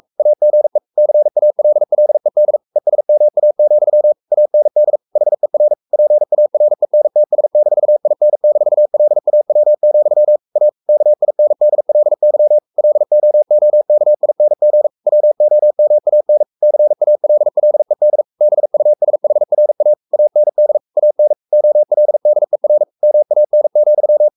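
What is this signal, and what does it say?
Morse code, heard as a keyed tone, about 44 wpm, 600 Hz. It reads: AGE CALLED ESMA, AND HER PARENTS-IN-LAW, A KINDLY LOOKING WOMAN CALLED BUSHRA AND AN OLDER MAN,
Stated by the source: Morse code at 43 wpm sending Ken Follett's novel Never in English, from CW med Gnister.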